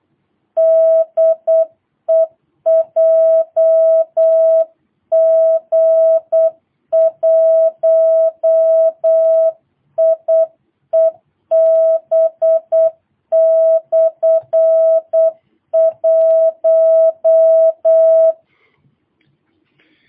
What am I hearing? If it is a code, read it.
Morse code, heard as a keyed tone, about 8 wpm, 650 Hz. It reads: DE JG1IEB/1
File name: jh8cbh / 2022cw